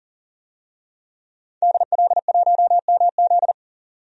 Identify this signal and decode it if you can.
Morse code, heard as a keyed tone, about 40 wpm, 700 Hz. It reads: DL1MZ
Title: DXCC-100 Call Signs - 5 Characters - 40wpm